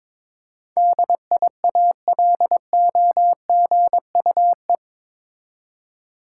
Morse code, heard as a keyed tone, about 22 wpm, 700 Hz. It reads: DIALOGUE
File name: practice / 22wpm / Eight Letter Words - 22wpm